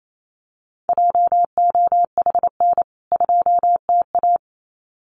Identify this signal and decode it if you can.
Morse code, heard as a keyed tone, about 28 wpm, 700 Hz. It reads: JOHN 2TA
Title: CWT Exchange 28wpm